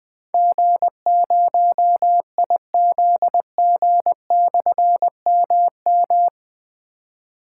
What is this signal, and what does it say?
Morse code, heard as a keyed tone, about 20 wpm, 700 Hz. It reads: G0IZG/MM
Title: Difficult Call Signs 20wpm